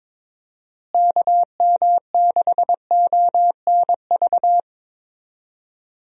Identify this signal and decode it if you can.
Morse code, heard as a keyed tone, about 21 wpm, 700 Hz. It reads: KM6ONV
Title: USA Call Signs - 6 Characters - Repeated 3x - 22wpm